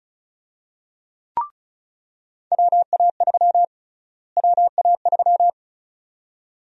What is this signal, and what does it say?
Morse code, heard as a keyed tone, about 36 wpm, 700 Hz. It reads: WA3 WA3
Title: USA Call Sign Prefixes - Repeated 2x - 35wpm